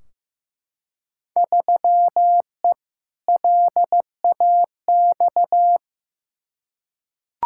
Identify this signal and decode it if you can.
Morse code, heard as a keyed tone, about 15 wpm, 700 Hz. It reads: 3E LAX
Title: Field Day Exchanges Encore - 15wpm